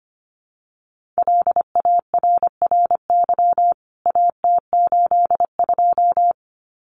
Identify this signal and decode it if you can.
Morse code, heard as a keyed tone, about 25 wpm, 700 Hz. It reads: LARRY AT82
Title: CWT Exchange 25wpm